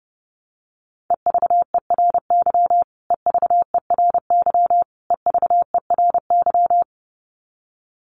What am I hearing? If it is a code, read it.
Morse code, heard as a keyed tone, about 28 wpm, 700 Hz. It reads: EVERY EVERY EVERY
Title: Top 300 Words Repeated 30wpm